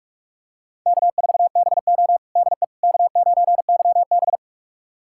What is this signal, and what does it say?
Morse code, heard as a keyed tone, about 44 wpm, 700 Hz. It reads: K4BO DE K9YB
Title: USA Callsign DE Callsign - 4 Characters - 45wpm